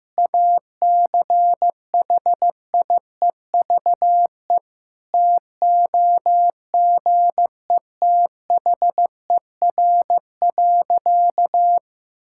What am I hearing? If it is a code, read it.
Morse code, heard as a keyed tone, about 16 wpm, 700 Hz. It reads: ACHIEVE TOGETHER.
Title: Sentences from Top 1000 Words - Review Part 6 - 15wpm